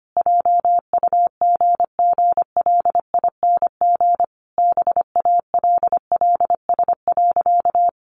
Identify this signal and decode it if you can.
Morse code, heard as a keyed tone, about 25 wpm, 700 Hz. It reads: JUGGLING BALLS.
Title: Easy Phrases 25wpm